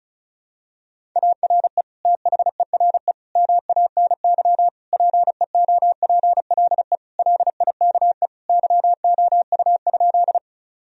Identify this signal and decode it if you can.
Morse code, heard as a keyed tone, about 36 wpm, 700 Hz. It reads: ARE THERE MANY PEOPLE LIKE YOU?
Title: Sentences from Top 1000 Words - Review Part 1 - 35wpm